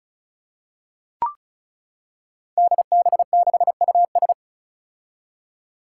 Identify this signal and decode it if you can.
Morse code, heard as a keyed tone, about 36 wpm, 700 Hz. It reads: DB6US
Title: Five Challenging Letter-Number Combinations 35wpm